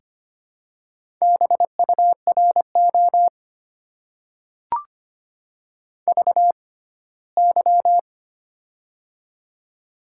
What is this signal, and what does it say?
Morse code, heard as a keyed tone, about 25 wpm, spaced as for 17 wpm, 700 Hz. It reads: BURO VY
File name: Top 90 QSO Elements - 6x Character Spacing - 25wpm